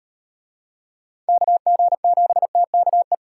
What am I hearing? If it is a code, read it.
Morse code, heard as a keyed tone, about 38 wpm, 700 Hz. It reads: KG7TKE